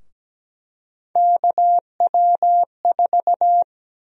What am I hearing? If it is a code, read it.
Morse code, heard as a keyed tone, about 17 wpm, 700 Hz. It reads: KW4
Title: USA Call Sign Prefixes - 17wpm